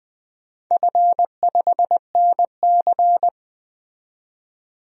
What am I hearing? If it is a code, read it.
Morse code, heard as a keyed tone, about 20 wpm, 700 Hz. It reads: F5NC
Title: DXCC-100 Call Signs - 4 Characters - 20wpm